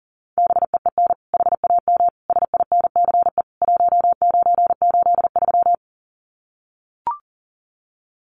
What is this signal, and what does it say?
Morse code, heard as a keyed tone, about 40 wpm, 700 Hz. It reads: BEEN HAM SINCE 1983
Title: Common QSO Phrases 40wpm